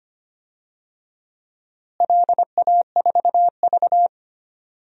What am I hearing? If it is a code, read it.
Morse code, heard as a keyed tone, about 25 wpm, 700 Hz. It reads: LA4V
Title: DXCC-100 Call Signs - 4 Characters - Repeated 2x - 25wpm